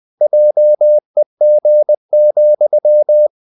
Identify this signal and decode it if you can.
Morse code, heard as a keyed tone, about 20 wpm, 600 Hz. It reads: JEG,
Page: Ferie 20 wpm